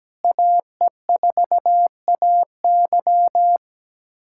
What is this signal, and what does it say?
Morse code, heard as a keyed tone, about 17 wpm, 700 Hz. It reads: AE4AY
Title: USA Callsign DE Callsign - 5 Characters - 17wpm